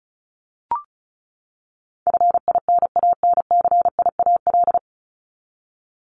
Morse code, heard as a keyed tone, about 34 wpm, 700 Hz. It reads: FINANCIAL
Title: Top 751-1000 Words 35wpm